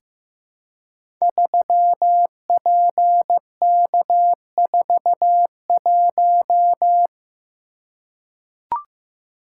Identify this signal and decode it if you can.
Morse code, heard as a keyed tone, about 15 wpm, 700 Hz. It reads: 3PK41
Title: Code Groups - 5 Letter:Numbers - 15wpm